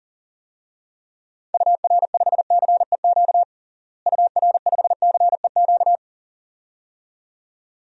Morse code, heard as a keyed tone, about 40 wpm, 700 Hz. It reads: UR5CEQ UR5CEQ
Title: DXCC-100 Call Signs - 6 Characters - Repeated 2x - 40wpm